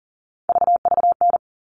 Morse code, heard as a keyed tone, about 40 wpm, 700 Hz. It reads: VVN
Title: All Cut Numbers - 3-Digit 40wpm